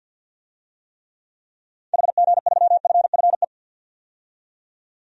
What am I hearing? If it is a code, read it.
Morse code, heard as a keyed tone, about 51 wpm, 700 Hz. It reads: HG3FFE
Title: DXCC-100 Call Signs - 6 Characters - 50wpm